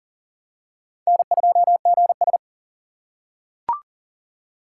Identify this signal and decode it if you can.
Morse code, heard as a keyed tone, about 40 wpm, 700 Hz. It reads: N2GS
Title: USA Call Signs - 2x Character Spacing - 40wpm